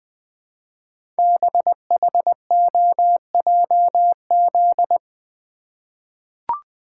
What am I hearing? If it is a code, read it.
Morse code, heard as a keyed tone, about 20 wpm, 700 Hz. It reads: BHOJZ
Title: Code Groups - 5 Letters - 20wpm